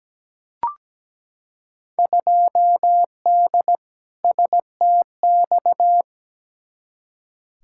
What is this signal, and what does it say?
Morse code, heard as a keyed tone, about 17 wpm, 700 Hz. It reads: 2D STX
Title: Field Day Exchanges - 17wpm